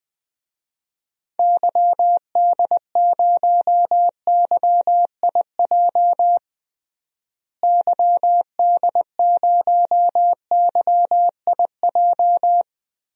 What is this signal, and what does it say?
Morse code, heard as a keyed tone, about 21 wpm, 700 Hz. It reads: YD0YIJ YD0YIJ
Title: DXCC-100 Call Signs - 6 Characters - Repeated 3x - 20wpm